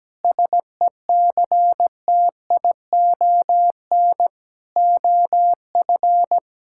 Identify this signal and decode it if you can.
Morse code, heard as a keyed tone, about 17 wpm, 700 Hz. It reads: SECTION OF
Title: Sentences from Top 1000 Words - Review Part 5 - 17wpm